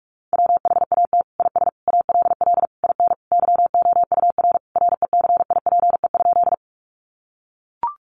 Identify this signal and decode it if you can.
Morse code, heard as a keyed tone, about 44 wpm, 700 Hz. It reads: WHAT IS ALL IN YOUR RECIPE?
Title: Sentences from Top 1200 Words 45wpm